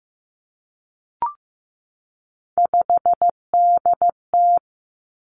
Morse code, heard as a keyed tone, about 15 wpm, 700 Hz. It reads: 5DT
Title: Three Challenging Letter-Number Combinations 15wpm